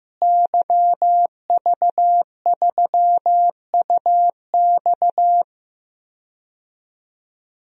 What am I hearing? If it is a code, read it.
Morse code, heard as a keyed tone, about 15 wpm, 700 Hz. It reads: YV3UX
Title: DXCC-100 Call Signs - 5 Characters - Repeated 3x - 15wpm